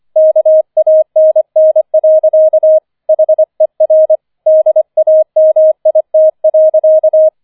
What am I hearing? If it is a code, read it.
Morse code, heard as a keyed tone, about 25 wpm, 630 Hz. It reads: KANN. HER DAMIT.